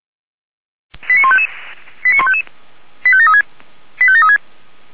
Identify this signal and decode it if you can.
Five-tone selective call (ZVEI), heard as quick runs of five tones, tones of about 70 ms. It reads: 89155 89155 85736 85736
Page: SDR/ZVEI